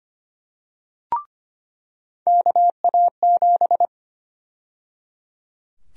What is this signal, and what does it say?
Morse code, heard as a keyed tone, about 25 wpm, 700 Hz. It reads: KA7
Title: USA Call Sign Prefixes - 25wpm